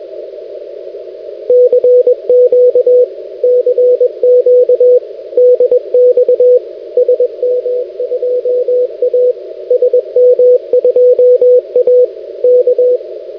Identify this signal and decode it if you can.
Morse code, heard as a keyed tone, about 21 wpm, 500 Hz, with radio noise. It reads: CQ CQ DX SM2A SM2A K
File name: 6m-sm2a-2nd